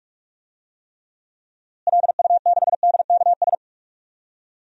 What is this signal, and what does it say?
Morse code, heard as a keyed tone, about 42 wpm, 700 Hz. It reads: LU6DKS
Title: DXCC-100 Call Signs - 6 Characters - Repeated 3x - 45wpm